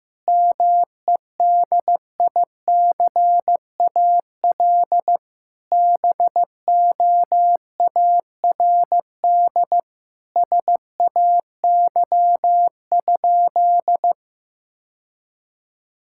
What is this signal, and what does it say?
Morse code, heard as a keyed tone, about 15 wpm, 700 Hz. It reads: MEDICAL BOARD SAY?
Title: Sentences from Top 1000 Words - Review Part 5 - 15wpm